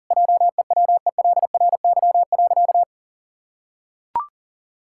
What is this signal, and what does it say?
Morse code, heard as a keyed tone, about 37 wpm, 700 Hz. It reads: JEWELRY.
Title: Pangram Challenge v2 - 40wpm